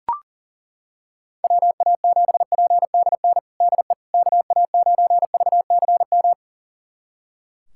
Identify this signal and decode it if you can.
Morse code, heard as a keyed tone, about 37 wpm, 700 Hz. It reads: WA7PDN DE KA9VCM
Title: USA Callsign DE Callsign 40wpm